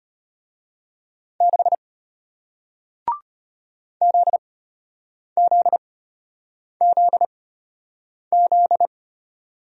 Morse code, heard as a keyed tone, about 34 wpm, spaced as for 8 wpm, 690 Hz. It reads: 6 ZZZZ